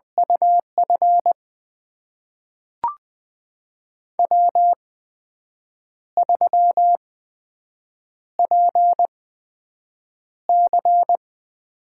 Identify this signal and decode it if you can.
Morse code, heard as a keyed tone, about 20 wpm, spaced as for 7 wpm, 700 Hz. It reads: UF W3PC
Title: USA Call Signs - 8x Character Spacing - 20wpm